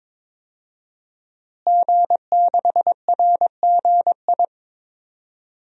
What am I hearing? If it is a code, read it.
Morse code, heard as a keyed tone, about 22 wpm, 700 Hz. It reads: G6RGI
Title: DXCC-100 Call Signs - 5 Characters - Repeated 2x - 22wpm